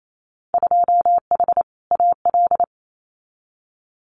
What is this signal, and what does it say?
Morse code, heard as a keyed tone, about 28 wpm, 700 Hz. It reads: 2H AL
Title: Winter Field Day Exchanges 28wpm